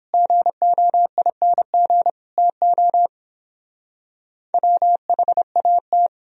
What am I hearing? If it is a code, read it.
Morse code, heard as a keyed tone, about 29 wpm, 700 Hz. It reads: GOING TO WHAT